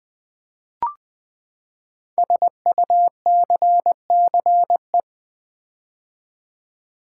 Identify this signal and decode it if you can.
Morse code, heard as a keyed tone, about 20 wpm, 700 Hz. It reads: SUCCE
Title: Build A Word - 5 Letters At A Time - 10-Letter Words - 20wpm